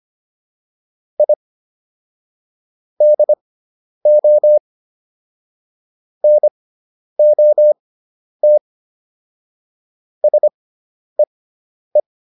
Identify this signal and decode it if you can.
Morse code, heard as a keyed tone, about 25 wpm, spaced as for 10 wpm, 600 Hz. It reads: I DO NOT SEE